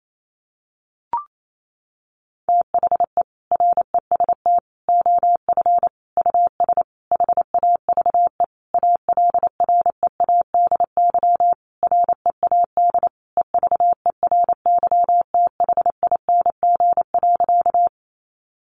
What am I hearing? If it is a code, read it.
Morse code, heard as a keyed tone, about 28 wpm, 700 Hz. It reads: THE REST OF US HAVE ALREADY READ EVERYTHING.
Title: Sentences from Top 700 Words 28wpm